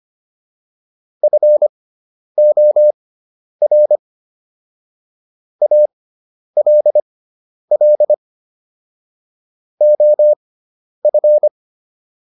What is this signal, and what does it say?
Morse code, heard as a keyed tone, about 25 wpm, spaced as for 10 wpm, 600 Hz. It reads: FOR ALL OF